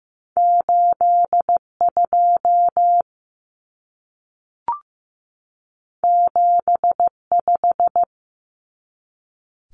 Morse code, heard as a keyed tone, about 15 wpm, 700 Hz. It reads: 8 2 7 5
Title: Common Cut Numbers - 2-Digit 15wpm